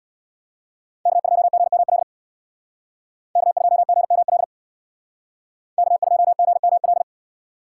Morse code, heard as a keyed tone, about 61 wpm, 700 Hz. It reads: K2GML K2GML K2GML
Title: Speed Racing - USA 1x3 Call Signs 50wpm